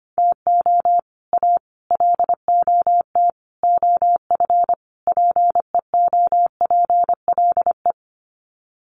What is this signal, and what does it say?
Morse code, heard as a keyed tone, about 25 wpm, 700 Hz. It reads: TO A LOT OF PEOPLE
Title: Sets of 5 Words Encore 25wpm